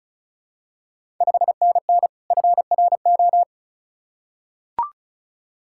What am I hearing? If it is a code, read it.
Morse code, heard as a keyed tone, about 36 wpm, 690 Hz. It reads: 5NN FRO
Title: Ontario QSO Party Exchange 35wpm